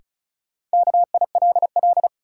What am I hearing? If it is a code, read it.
Morse code, heard as a keyed tone, about 36 wpm, 700 Hz. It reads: KILL
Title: Sentences from Top 1400 Words - New Word Drill - 35wpm